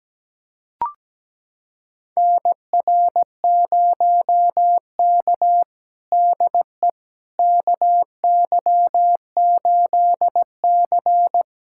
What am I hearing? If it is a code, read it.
Morse code, heard as a keyed tone, about 17 wpm, 700 Hz. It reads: NR0K DE KY8C
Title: USA Callsign DE Callsign - 4 Characters - 17wpm